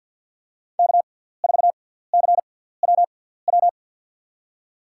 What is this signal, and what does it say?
Morse code, heard as a keyed tone, about 47 wpm, spaced as for 17 wpm, 700 Hz. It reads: K4CWW
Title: USA Call Signs - 5 Characters - 6x Character Spacing - 50wpm